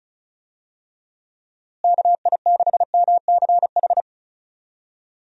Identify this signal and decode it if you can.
Morse code, heard as a keyed tone, about 34 wpm, 700 Hz. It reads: KI6MCH
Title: USA Call Signs - 6 Characters - Repeated 2x - 35wpm